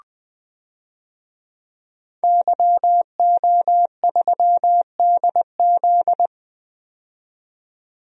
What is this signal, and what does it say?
Morse code, heard as a keyed tone, about 20 wpm, 700 Hz. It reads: YO3DZ
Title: DXCC-100 Call Signs - 5 Characters - 20wpm